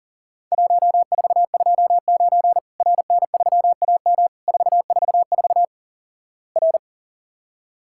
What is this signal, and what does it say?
Morse code, heard as a keyed tone, about 43 wpm, 700 Hz. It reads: 1429 RN3AM 444 R
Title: Worked All Europe DX Contest - QTC Exchange - 40wpm